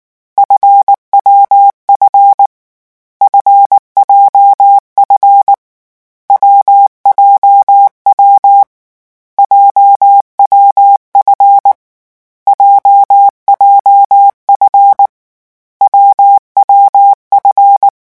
Morse code, heard as a keyed tone, about 19 wpm, 800 Hz. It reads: FWF FJF WJW JWF JJF WWF
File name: morse mp3 / lecon7-vitesse 18